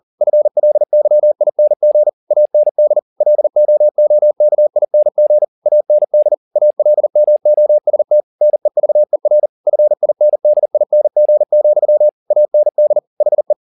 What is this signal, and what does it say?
Morse code, heard as a keyed tone, about 40 wpm, 600 Hz. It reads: FLYING AND LOOKING AND ALMOST NEVER FINDING, AND HE